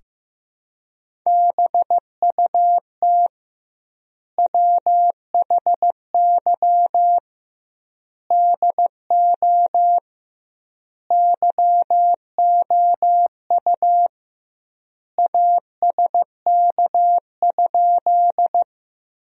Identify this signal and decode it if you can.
Morse code, heard as a keyed tone, about 15 wpm, 700 Hz. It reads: BUT WHY DO YOU ASK?
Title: 2x Word-Spacing - Sentences from Top 200 Words - 15wpm